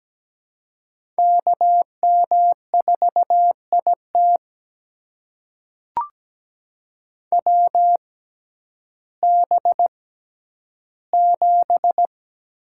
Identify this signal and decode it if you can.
Morse code, heard as a keyed tone, about 17 wpm, spaced as for 10 wpm, 700 Hz. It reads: KM4IT WB7
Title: USA Call Signs - 6x Character Spacing - 17wpm